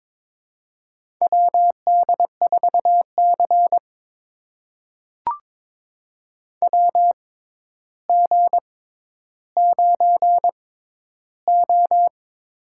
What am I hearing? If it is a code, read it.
Morse code, heard as a keyed tone, about 21 wpm, spaced as for 11 wpm, 700 Hz. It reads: WD4C WG9O
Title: USA Call Signs - 4 Characters - 6x Character Spacing - 22wpm